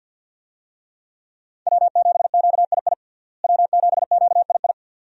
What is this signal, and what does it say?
Morse code, heard as a keyed tone, about 50 wpm, 700 Hz. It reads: W7QII W7QII
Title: USA Call Signs - 5 Characters - Repeated 2x - 50wpm